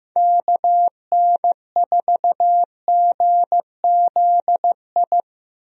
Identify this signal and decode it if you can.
Morse code, heard as a keyed tone, about 15 wpm, 700 Hz. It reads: KN4GZI
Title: USA Call Signs - 6 Characters - Repeated 3x - 15wpm